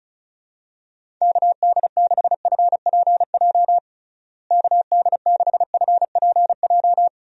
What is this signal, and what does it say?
Morse code, heard as a keyed tone, about 34 wpm, 700 Hz. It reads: KD6FPJ KD6FPJ
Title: USA Call Signs - 6 Characters - Repeated 3x - 35wpm